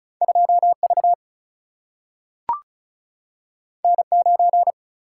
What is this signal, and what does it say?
Morse code, heard as a keyed tone, about 34 wpm, 700 Hz. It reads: 2V N9
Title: Single-Letter Single-Number 35wpm